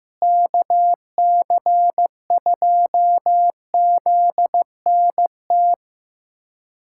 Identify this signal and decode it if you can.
Morse code, heard as a keyed tone, about 15 wpm, 700 Hz. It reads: KC2ZNT
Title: USA Callsign DE Callsign 15wpm